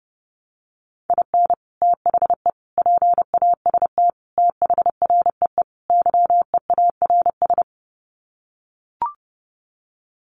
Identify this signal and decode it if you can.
Morse code, heard as a keyed tone, about 30 wpm, 700 Hz. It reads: IN THE PAST THREE YEARS
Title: Sets of 5 Words Encore2 30wpm